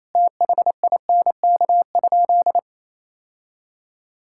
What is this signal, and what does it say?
Morse code, heard as a keyed tone, about 28 wpm, 700 Hz. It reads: THINK?
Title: Sentences from Top 1000 Words - Review Part 5 - 28wpm